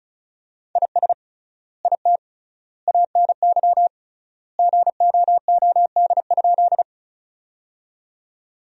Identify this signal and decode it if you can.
Morse code, heard as a keyed tone, about 36 wpm, 700 Hz. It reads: IS IT ANY GOOD?
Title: 3x Word-Spacing - Sentences from Top 200 Words - 35wpm